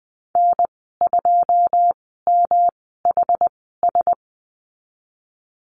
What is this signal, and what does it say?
Morse code, heard as a keyed tone, about 20 wpm, spaced as for 15 wpm, 700 Hz. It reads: N2MHS
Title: 122 Lesson 20g - Number 3 - Call Signs 20wpm